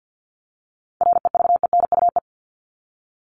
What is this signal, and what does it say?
Morse code, heard as a keyed tone, about 48 wpm, 700 Hz. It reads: REVENUE
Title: Top 1501-1750 Words 50wpm